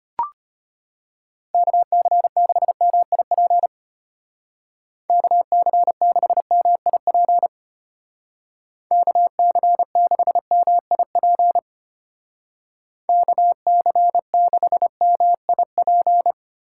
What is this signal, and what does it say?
Morse code, heard as a keyed tone, about 32 wpm, 700 Hz. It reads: KC6MIP KC6MIP KC6MIP KC6MIP